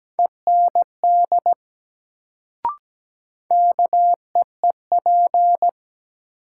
Morse code, heard as a keyed tone, about 17 wpm, 700 Hz. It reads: END KEEP
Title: Sets of 2 Words - New Word Drill - 17wpm